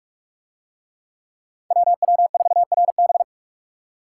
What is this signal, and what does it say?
Morse code, heard as a keyed tone, about 44 wpm, 700 Hz. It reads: WW4RB